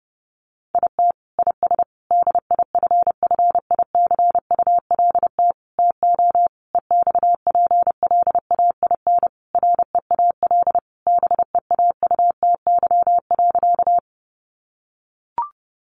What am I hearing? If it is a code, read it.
Morse code, heard as a keyed tone, about 30 wpm, 700 Hz. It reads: IT IS DIFFICULT TO EXPLAIN REAL BEAUTY.
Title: Sentences from Top 1100 Words 30wpm